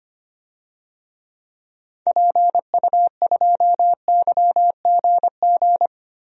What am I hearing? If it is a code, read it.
Morse code, heard as a keyed tone, about 25 wpm, 700 Hz. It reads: PU2YGG
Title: DXCC-100 Call Signs - 6 Characters - Repeated 3x - 25wpm